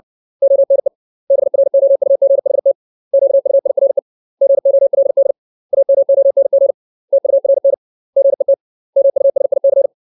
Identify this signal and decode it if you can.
Morse code, heard as a keyed tone, about 60 wpm, 550 Hz. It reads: ONE BROUGHT QUIRE GOLD AMONG TURN GET MUSIC
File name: 632 words / 632 Words 60WPM